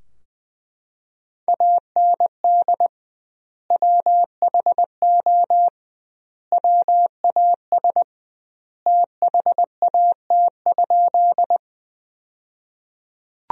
Morse code, heard as a keyed tone, about 20 wpm, 700 Hz. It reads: AND WHO WAS THAT?
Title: 2x Word-Spacing - Sentences from Top 100 Words - 20wpm